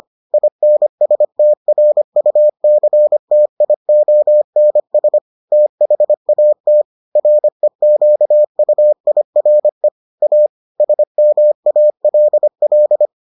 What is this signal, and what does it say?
Morse code, heard as a keyed tone, about 25 wpm, 600 Hz. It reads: INSTRUCTIONS THAT REQUIRE A SMALL